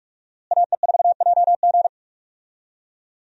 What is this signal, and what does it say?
Morse code, heard as a keyed tone, about 46 wpm, 700 Hz. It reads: AE4JG
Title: USA Call Signs - 5 Characters - Repeated 3x - 45wpm